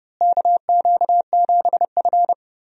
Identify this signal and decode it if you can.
Morse code, heard as a keyed tone, about 30 wpm, 700 Hz. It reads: KQ7F
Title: USA Call Signs - 4 Characters - Repeated 2x - 30wpm